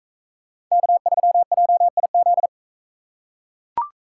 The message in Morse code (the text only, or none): K3JIZ